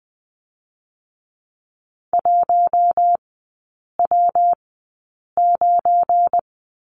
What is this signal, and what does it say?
Morse code, heard as a keyed tone, about 20 wpm, spaced as for 8 wpm, 700 Hz. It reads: 1W9
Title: 172 Lesson 28e - Number 9 - Three Characters 20wpm